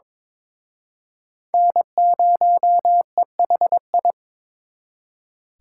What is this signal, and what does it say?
Morse code, heard as a keyed tone, about 26 wpm, 700 Hz. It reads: N0EHI
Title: Speed Racing - USA 1x3 Call Signs 22wpm